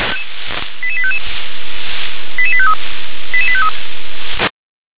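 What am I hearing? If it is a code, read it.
Five-tone selective call (ZVEI), heard as quick runs of five tones, tones of about 70 ms. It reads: 88855 88853 88853
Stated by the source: ZVEI five-tone selective call received by SDR.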